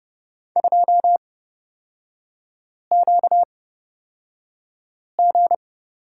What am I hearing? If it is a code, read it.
Morse code, heard as a keyed tone, about 30 wpm, spaced as for 5 wpm, 700 Hz. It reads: 2QG
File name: Lesson 38e - Letter q - Three Characters 30wpm